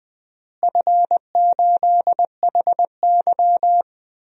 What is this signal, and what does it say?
Morse code, heard as a keyed tone, about 20 wpm, 700 Hz. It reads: F8HY
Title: DXCC-100 Call Signs - 4 Characters - Repeated 3x - 20wpm